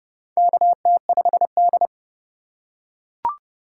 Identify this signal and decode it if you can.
Morse code, heard as a keyed tone, about 29 wpm, 690 Hz. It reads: KT5D
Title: USA Call Signs - 4 Characters - 2x Character Spacing - 30wpm